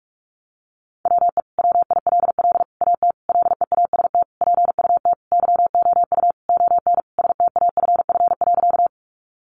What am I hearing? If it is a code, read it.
Morse code, heard as a keyed tone, about 44 wpm, 700 Hz. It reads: WE WILL AT LEAST PUT YOU ON STAFF.
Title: Sentences from Top 1200 Words 45wpm